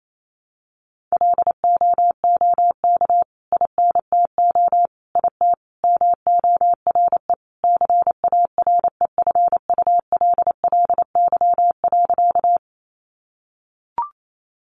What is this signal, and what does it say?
Morse code, heard as a keyed tone, about 28 wpm, 700 Hz. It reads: LOOK INTO IT MORE CAREFULLY.
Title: Sentences from Top 1300 Words 28wpm